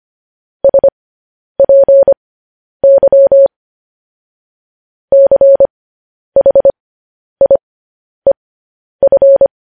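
Morse code, heard as a keyed tone, about 25 wpm, spaced as for 10 wpm, 550 Hz. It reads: SPY CHIEF
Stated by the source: News headlines sent in Morse at 10 wpm.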